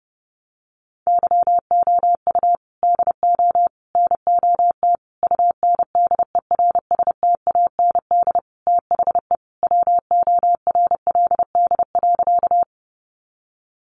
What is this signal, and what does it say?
Morse code, heard as a keyed tone, about 30 wpm, 700 Hz. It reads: YOU DO NOT UNDERSTAND THE WORLD.
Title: Sentences from Top 600 Words 30wpm